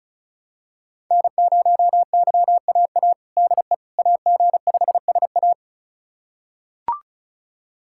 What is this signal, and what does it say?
Morse code, heard as a keyed tone, about 36 wpm, 700 Hz. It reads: N0YAA DE AG5SA
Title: USA Callsign DE Callsign 35wpm